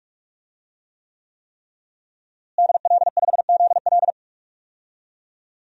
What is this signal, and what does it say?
Morse code, heard as a keyed tone, about 46 wpm, 700 Hz. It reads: DL5ZL
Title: DXCC-100 Call Signs - 5 Characters - Repeated 3x - 45wpm